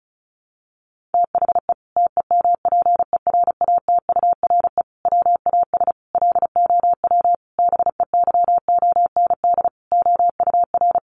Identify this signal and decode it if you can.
Morse code, heard as a keyed tone, about 34 wpm, 700 Hz. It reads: THE TEMPERATURE WAS LOW BEYOND OUR
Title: Sentences from Top 1500 Words 35wpm